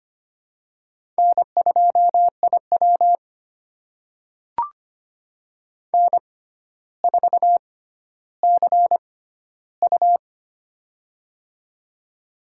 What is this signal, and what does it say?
Morse code, heard as a keyed tone, about 25 wpm, spaced as for 13 wpm, 700 Hz. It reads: N2IW N4CU
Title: USA Call Signs - 4 Characters - 6x Character Spacing - 25wpm